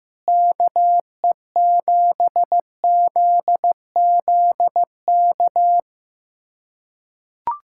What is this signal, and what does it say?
Morse code, heard as a keyed tone, about 15 wpm, 700 Hz. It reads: KE7ZZK
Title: USA Callsign DE Callsign - 6 Characters - 15wpm